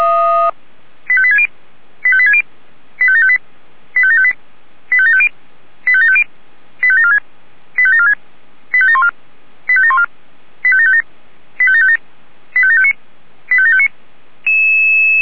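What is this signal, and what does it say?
Five-tone selective call (ZVEI), heard as quick runs of five tones, tones of about 70 ms. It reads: 85860 85860 85758 85758 85750 85750 85746 85746 85714 85714 85757 85757 85759 85759
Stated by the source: ZVEI five-tone call 85721 received by SDR.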